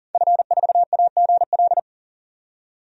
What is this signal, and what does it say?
Morse code, heard as a keyed tone, about 40 wpm, 700 Hz. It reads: F4AGL
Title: DXCC-100 Call Signs - 5 Characters - 40wpm